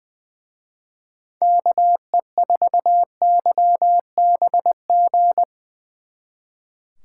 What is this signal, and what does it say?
Morse code, heard as a keyed tone, about 23 wpm, 700 Hz. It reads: KE4YBG
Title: Speed Racing - USA 2x3 Call Signs 20wpm